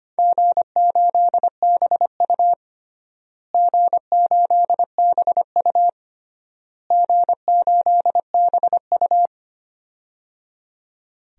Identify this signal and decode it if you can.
Morse code, heard as a keyed tone, about 25 wpm, 700 Hz. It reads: G8BU G8BU G8BU
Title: DXCC-100 Call Signs - 4 Characters - Repeated 3x - 25wpm